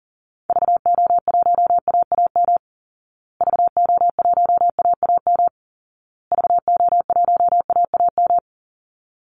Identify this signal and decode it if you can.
Morse code, heard as a keyed tone, about 40 wpm, 700 Hz. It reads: VO1AAM VO1AAM VO1AAM
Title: Call Signs Repeated 2x - 40wpm